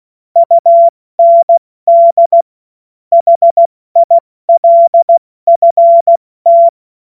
Morse code, heard as a keyed tone, about 16 wpm, 680 Hz. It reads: UND HILFT
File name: Morsekurs / Witze / Witz 20. Januar 2026 70 F80